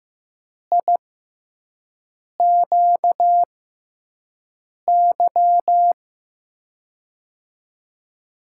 Morse code, heard as a keyed tone, about 15 wpm, spaced as for 5 wpm, 700 Hz. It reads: IQY
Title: USA Call Signs - 6 Characters - 6x Character Spacing - 15wpm